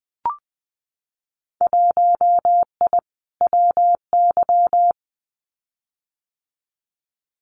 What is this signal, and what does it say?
Morse code, heard as a keyed tone, about 20 wpm, 700 Hz. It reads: 1I WY
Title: Winter Field Day Exchanges Encore 20wpm